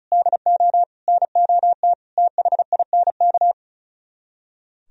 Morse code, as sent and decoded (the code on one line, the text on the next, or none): -.. --- / -. --- - / - .... .. -. -.-
DO NOT THINK